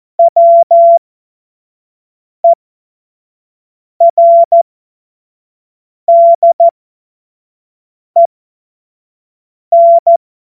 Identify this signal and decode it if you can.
Morse code, heard as a keyed tone, about 14 wpm, spaced as for 5 wpm, 680 Hz. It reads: WERDEN
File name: Morsekurs / Witze / Witz 29. November 2022 25 F70